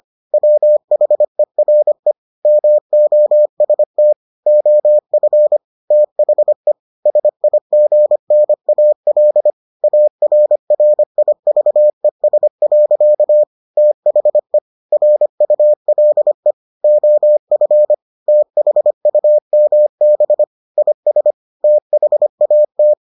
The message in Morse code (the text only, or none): WHERE MOST OF THE SIGNAL ARRIVES. THE RULE OF THUMB IS THAT